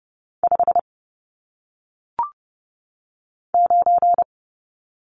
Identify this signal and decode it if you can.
Morse code, heard as a keyed tone, about 30 wpm, 700 Hz. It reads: 5 9
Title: Numbers - 1-Digit 30wpm